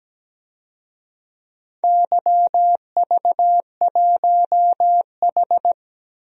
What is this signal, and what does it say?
Morse code, heard as a keyed tone, about 17 wpm, 700 Hz. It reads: YV1H
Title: DXCC-100 Call Signs - 4 Characters - 17wpm